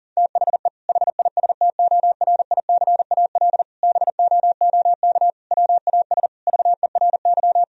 Morse code, heard as a keyed tone, about 40 wpm, 690 Hz. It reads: THE HISTORICAL BOOK WAS VERY
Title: Sentences from Top 1000 Words - Review Part 4 - 40wpm